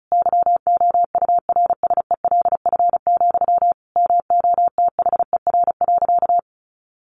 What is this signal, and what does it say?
Morse code, heard as a keyed tone, about 34 wpm, 700 Hz. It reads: YOURSELF, MOTHER.
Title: Sentences from Top 1300 Words 35wpm